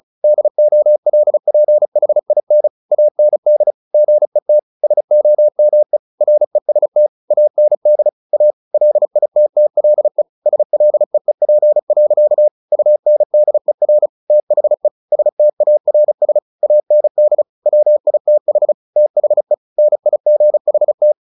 Morse code, heard as a keyed tone, about 34 wpm, 600 Hz. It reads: DOLPHIN AND GET SOME REST AND A LITTLE SLEEP. UNDER THE STARS AND WITH THE NIGHT
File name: the old man and the sea 2 35wpm